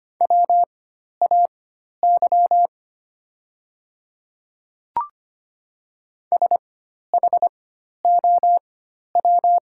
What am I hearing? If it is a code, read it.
Morse code, heard as a keyed tone, about 25 wpm, spaced as for 12 wpm, 700 Hz. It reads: WAY SHOW
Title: 4x Character Spacing - Top 200 Words - 25wpm